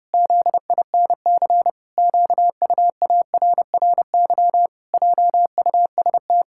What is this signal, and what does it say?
Morse code, heard as a keyed tone, about 32 wpm, 700 Hz. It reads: ZINC QUARRY JUST
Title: Pangram Challenge v2 - 30wpm